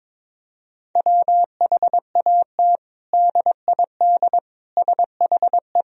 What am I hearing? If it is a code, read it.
Morse code, heard as a keyed tone, about 22 wpm, 700 Hz. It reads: WHAT DID SHE
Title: Sentences from Top 1000 Words - Review Part 4 - 22wpm